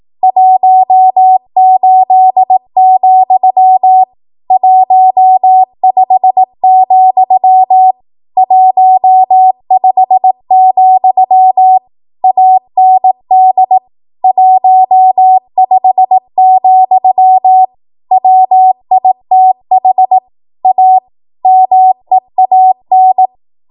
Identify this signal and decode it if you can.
Morse code, heard as a keyed tone, about 18 wpm, 750 Hz. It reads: 18, 15, 15, AND 15, WITH A MEAN
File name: Morse / 18WPM